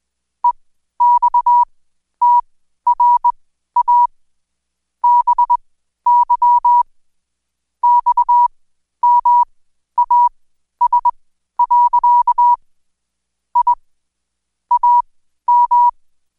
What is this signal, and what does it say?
Morse code, heard as a keyed tone, about 20 wpm, spaced as for 12 wpm, 970 Hz. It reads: EXTRA BY XMAS. I AM
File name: Code Quick / AudioDownld2 / Seg121-Day26-DayDrill 26